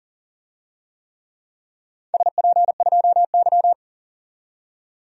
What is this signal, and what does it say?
Morse code, heard as a keyed tone, about 40 wpm, 700 Hz. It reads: SP2Y
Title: DXCC-100 Call Signs - 4 Characters - 40wpm